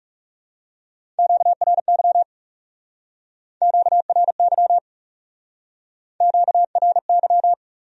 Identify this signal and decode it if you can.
Morse code, heard as a keyed tone, about 39 wpm, 700 Hz. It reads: QRY QRY QRY